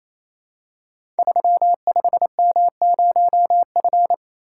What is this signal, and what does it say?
Morse code, heard as a keyed tone, about 28 wpm, 700 Hz. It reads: 35M0F
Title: Code Groups - 5 Letter:Numbers - 28wpm